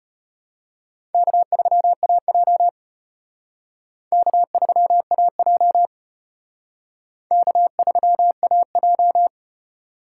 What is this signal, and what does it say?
Morse code, heard as a keyed tone, about 32 wpm, 700 Hz. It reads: K3AJ K3AJ K3AJ